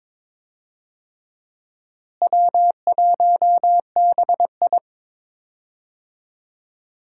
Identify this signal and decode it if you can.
Morse code, heard as a keyed tone, about 21 wpm, 700 Hz. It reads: W1BI
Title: USA Call Signs - 4 Characters - Repeated 3x - 22wpm